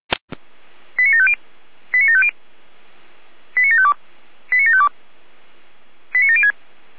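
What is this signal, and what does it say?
Five-tone selective call (ZVEI), heard as quick runs of five tones, tones of about 80 ms. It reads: 89750 89750 89752 89752 89706
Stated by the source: ZVEI five-tone call 89581 received on SDR.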